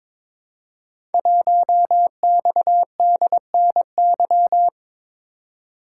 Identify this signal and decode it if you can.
Morse code, heard as a keyed tone, about 22 wpm, 700 Hz. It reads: 1XDNY
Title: Code Groups - 5 Letter:Numbers - 22wpm